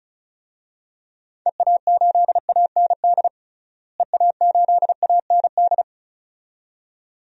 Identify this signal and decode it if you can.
Morse code, heard as a keyed tone, about 34 wpm, 700 Hz. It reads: EA8AND EA8AND
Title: DXCC-100 Call Signs - 6 Characters - Repeated 2x - 35wpm